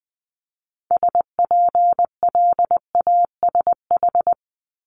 Morse code, heard as a keyed tone, about 20 wpm, 700 Hz. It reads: SPLASH
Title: Multi-letter Phonemes Repeated 3x 20wpm